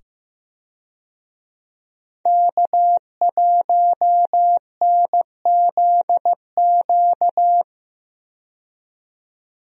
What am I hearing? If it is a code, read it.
Morse code, heard as a keyed tone, about 15 wpm, 700 Hz. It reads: K1NZQ
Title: USA Call Signs - 5 Characters - 15wpm